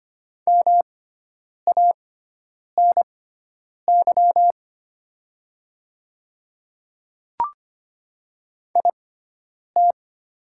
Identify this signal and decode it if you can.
Morse code, heard as a keyed tone, about 25 wpm, spaced as for 9 wpm, 700 Hz. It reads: MANY IT
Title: 6x Character Spacing - Top 300 Words - 25wpm